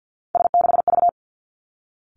Morse code, heard as a keyed tone, about 50 wpm, 700 Hz. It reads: S6V
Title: Three Challenging Letter-Number Combinations 50wpm